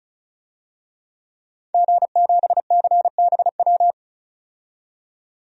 Morse code, heard as a keyed tone, about 36 wpm, 700 Hz. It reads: G7CBW
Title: DXCC-100 Call Signs - 5 Characters - 35wpm